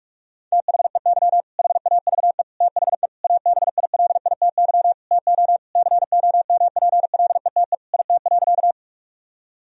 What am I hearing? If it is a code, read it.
Morse code, heard as a keyed tone, about 46 wpm, 700 Hz. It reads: THEY HAVE THE ABILITY TO COMPLETE IT.
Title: Sentences from Top 1000 Words - Review Part 3 - 45wpm